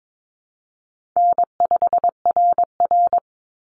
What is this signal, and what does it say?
Morse code, heard as a keyed tone, about 22 wpm, 700 Hz. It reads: N5RR
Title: Easy Call Signs - 22wpm